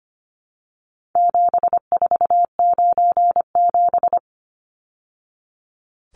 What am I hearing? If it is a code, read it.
Morse code, heard as a keyed tone, about 25 wpm, 700 Hz. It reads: 7497
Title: Numbers - 4-Digit 25wpm